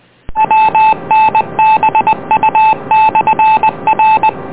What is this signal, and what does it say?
Morse code, heard as a keyed tone, about 20 wpm, 870 Hz, with noise. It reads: WNBU/R